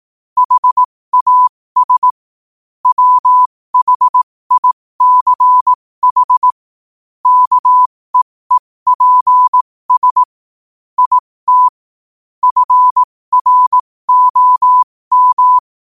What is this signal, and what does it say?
Morse code, heard as a keyed tone, about 18 wpm, spaced as for 15 wpm, 1000 Hz. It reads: HAS WHICH KEEPS IT FROM
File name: qotdfiles / 15wpm